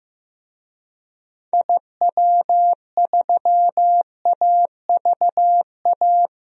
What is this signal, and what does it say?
Morse code, heard as a keyed tone, about 16 wpm, 700 Hz. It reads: IW3AVA
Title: DXCC-100 Call Signs - 6 Characters - Repeated 2x - 15wpm